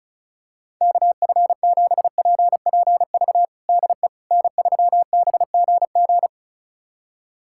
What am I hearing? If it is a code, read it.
Morse code, heard as a keyed tone, about 36 wpm, 700 Hz. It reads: KF7PPV DE N3BGG
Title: USA Callsign DE Callsign 35wpm